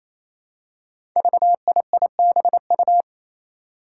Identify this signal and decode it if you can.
Morse code, heard as a keyed tone, about 28 wpm, 700 Hz. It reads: VIIBU
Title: Five Challenging Letter-Number Combinations 28wpm